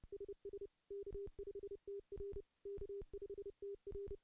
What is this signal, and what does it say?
Morse code, heard as a keyed tone, about 30 wpm, 400 Hz. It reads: SS K5TR K5TR
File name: contests / CW-SS-k5tr- -0600